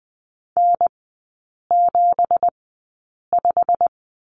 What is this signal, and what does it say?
Morse code, heard as a keyed tone, about 20 wpm, spaced as for 8 wpm, 700 Hz. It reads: N75
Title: 198 Lesson 32e - Letter k - Three Characters 20wpm